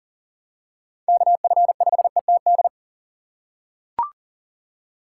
Morse code, heard as a keyed tone, about 38 wpm, 700 Hz. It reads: KF5ETD
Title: K1USN SST QSO Practice - 40wpm